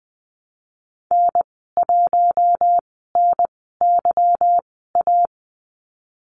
Morse code, heard as a keyed tone, about 20 wpm, spaced as for 15 wpm, 700 Hz. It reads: N1NYA
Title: 161 Lesson 26g - Letter g - Call Signs 20wpm